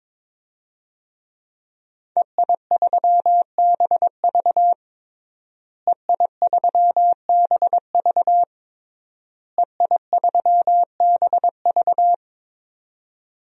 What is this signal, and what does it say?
Morse code, heard as a keyed tone, about 22 wpm, 700 Hz. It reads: EI3BV EI3BV EI3BV
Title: DXCC-100 Call Signs - 5 Characters - Repeated 3x - 22wpm